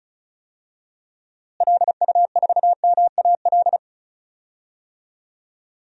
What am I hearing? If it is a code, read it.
Morse code, heard as a keyed tone, about 36 wpm, 700 Hz. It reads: LU4MAL
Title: DXCC-100 Call Signs - 6 Characters - Repeated 3x - 35wpm